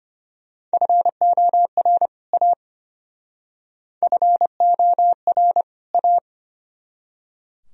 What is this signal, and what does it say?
Morse code, heard as a keyed tone, about 31 wpm, 700 Hz. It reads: FOR A FOR A
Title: Speed-Racing - Sets of 2 Words 25wpm